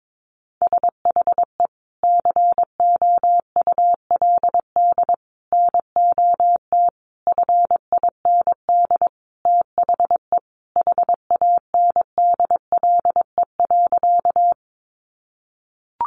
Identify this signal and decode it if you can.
Morse code, heard as a keyed tone, about 22 wpm, 700 Hz. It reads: SHE COULD NOT FIND THE HANDLE.
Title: Sentences from Top 1100 Words 22wpm